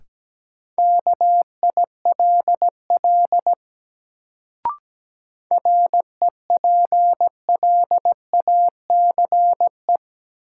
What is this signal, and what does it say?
Morse code, heard as a keyed tone, about 17 wpm, 700 Hz. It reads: KILL REPLACE
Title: Sentences from Top 1400 Words - New Word Drill - 17wpm